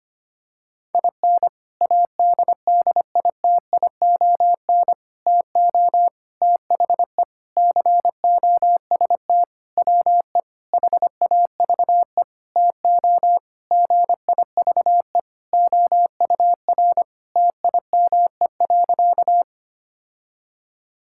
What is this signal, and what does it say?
Morse code, heard as a keyed tone, about 25 wpm, 700 Hz. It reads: IN ADDITION TO THE COST WE HAVE TO GIVE OUR TIME.
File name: practice / 25wpm / Sentences with 4-Word Phrases - 25wpm